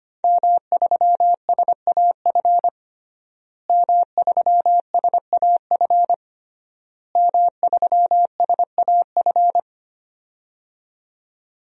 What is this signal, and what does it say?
Morse code, heard as a keyed tone, about 25 wpm, 700 Hz. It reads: M3SAF M3SAF M3SAF
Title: DXCC-100 Call Signs - 5 Characters - Repeated 3x - 25wpm